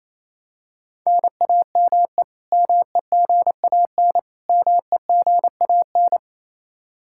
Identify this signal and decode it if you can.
Morse code, heard as a keyed tone, about 28 wpm, 700 Hz. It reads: NAME MEGAN MEGAN
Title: Practice QSO - Exchange 1 - Beginning - 28wpm